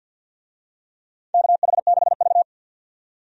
Morse code, heard as a keyed tone, about 46 wpm, 700 Hz. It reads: KH6V
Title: USA Call Signs - 4 Characters - 50wpm